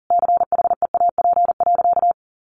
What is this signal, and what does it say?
Morse code, heard as a keyed tone, about 40 wpm, 700 Hz. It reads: CHEAP.
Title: Sentences from Top 1200 Words 40wpm